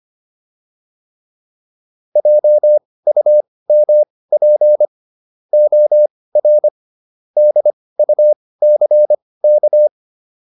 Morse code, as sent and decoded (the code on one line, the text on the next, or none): .--- ..- -- .--. / --- .-. / -.. ..- -.-. -.-
JUMP OR DUCK